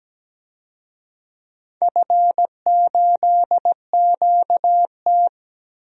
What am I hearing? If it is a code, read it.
Morse code, heard as a keyed tone, about 17 wpm, 700 Hz. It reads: F8QT
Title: DXCC-100 Call Signs - 4 Characters - Repeated 3x - 17wpm